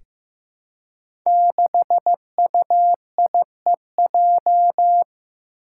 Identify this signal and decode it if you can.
Morse code, heard as a keyed tone, about 15 wpm, 700 Hz. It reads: 6UIEJ
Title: Five Challenging Letter-Number Combinations 15wpm